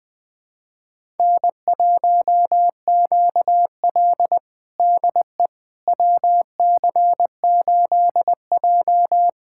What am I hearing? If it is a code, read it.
Morse code, heard as a keyed tone, about 20 wpm, 700 Hz. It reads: N1QL DE WC8J
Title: USA Callsign DE Callsign - 4 Characters - 20wpm